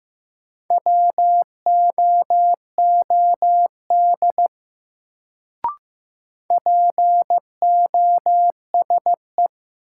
Morse code, heard as a keyed tone, about 15 wpm, 700 Hz. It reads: WOOD POSE
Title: Sentences from Top 1000 Words - Review Part 2 - New Word Drill - 15wpm